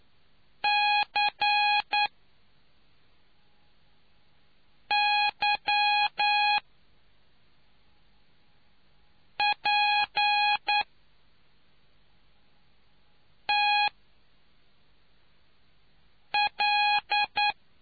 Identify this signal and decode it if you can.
Morse code, heard as a keyed tone, about 9 wpm, spaced as for 3 wpm, 830 Hz. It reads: CYPTL